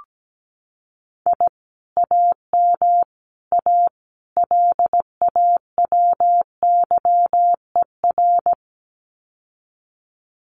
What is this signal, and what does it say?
Morse code, heard as a keyed tone, about 17 wpm, 700 Hz. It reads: I AM A LAWYER
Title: Common QSO Phrases Encore 17wpm